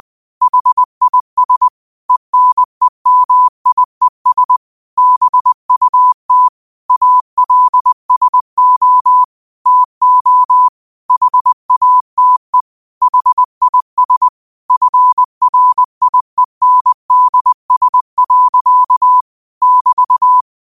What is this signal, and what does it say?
Morse code, heard as a keyed tone, about 20 wpm, 1000 Hz. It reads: HIS ENEMIES BUT ALSO TO HATE HIS FRIENDS. <BT>
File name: qotdfiles / 20wpm